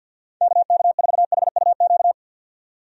Morse code, heard as a keyed tone, about 51 wpm, 700 Hz. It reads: KK4HUQ
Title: USA Call Signs - 6 Characters - 50wpm